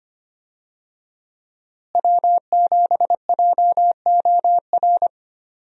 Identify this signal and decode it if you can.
Morse code, heard as a keyed tone, about 25 wpm, 700 Hz. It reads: W7JOR
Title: USA Call Signs - 5 Characters - Repeated 2x - 25wpm